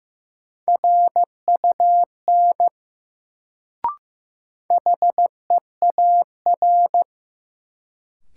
Morse code, heard as a keyed tone, about 15 wpm, 700 Hz. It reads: RUN HEAR
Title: Sentences from Top 300 Words - New Word Drill - 15wpm